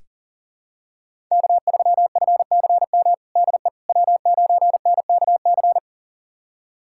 Morse code, heard as a keyed tone, about 40 wpm, 700 Hz. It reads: K3FCM DE W9NKC
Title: USA Callsign DE Callsign - 5 Characters - 40wpm